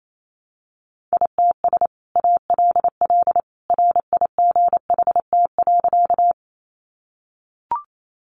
Morse code, heard as a keyed tone, about 28 wpm, 700 Hz. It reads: ITS ALL RIGHT.
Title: Sentences from Top 600 Words 28wpm